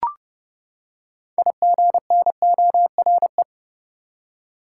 Morse code, heard as a keyed tone, about 31 wpm, 700 Hz. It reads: IGNORE
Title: Six Letter Words - 30wpm